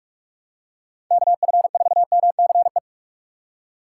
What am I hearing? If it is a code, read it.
Morse code, heard as a keyed tone, about 44 wpm, 700 Hz. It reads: KF4MCE